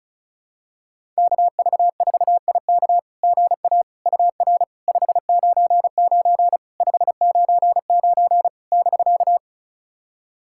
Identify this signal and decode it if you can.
Morse code, heard as a keyed tone, about 36 wpm, 690 Hz. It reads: KV4IK GA UR 599 599 <BK>
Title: SOTA QSO Practice 35wpm